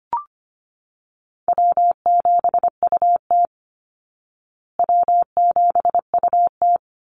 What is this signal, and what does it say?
Morse code, heard as a keyed tone, about 25 wpm, 700 Hz. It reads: W7UT W7UT
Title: Call Signs Repeated 2x - 25wpm